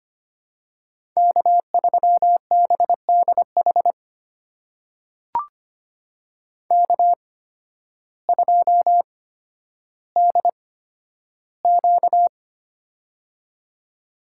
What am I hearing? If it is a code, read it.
Morse code, heard as a keyed tone, about 25 wpm, spaced as for 12 wpm, 700 Hz. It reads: K3BDH K2DQ
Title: USA Call Signs - 8x Character Spacing - 25wpm